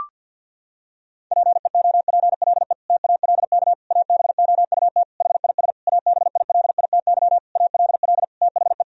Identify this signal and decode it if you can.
Morse code, heard as a keyed tone, about 52 wpm, 700 Hz. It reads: PEOPLE TALK ABOUT HIS ABILITY ALL THE
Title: Sentences from Top 1000 Words - Review Part 3 - 50wpm